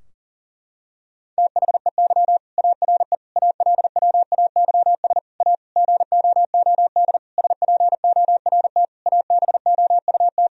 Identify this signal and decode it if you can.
Morse code, heard as a keyed tone, about 40 wpm, 700 Hz. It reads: THEY ARE ALWAYS A GOOD SPORT ABOUT
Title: Sentences from Top 1000 Words - Review Part 4 - 40wpm